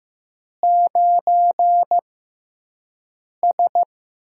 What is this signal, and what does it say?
Morse code, heard as a keyed tone, about 15 wpm, spaced as for 5 wpm, 700 Hz. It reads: 9S
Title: USA Call Signs - 4 Characters - 6x Character Spacing - 15wpm